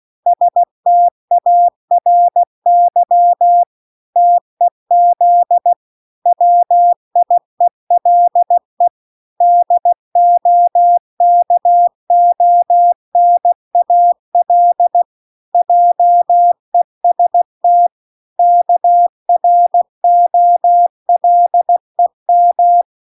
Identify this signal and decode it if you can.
Morse code, heard as a keyed tone, about 16 wpm, 700 Hz. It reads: STARY TEZ WIELE DOKONAL JEST KROLEM